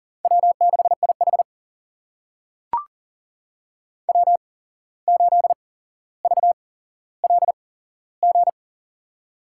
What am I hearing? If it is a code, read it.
Morse code, heard as a keyed tone, about 40 wpm, spaced as for 16 wpm, 700 Hz. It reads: W6IH W8VLG
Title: USA Call Signs - 8x Character Spacing - 40wpm